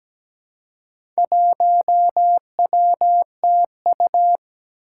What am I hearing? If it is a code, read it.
Morse code, heard as a keyed tone, about 17 wpm, 700 Hz. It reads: 1WTU
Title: Four Challenging Letter-Number Combinations 17wpm